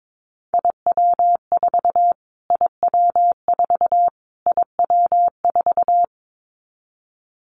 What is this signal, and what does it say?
Morse code, heard as a keyed tone, about 22 wpm, 700 Hz. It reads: IW4 IW4 IW4
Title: Callsign Prefixes for Top 25 DXCC - Repeated 3x - 22wpm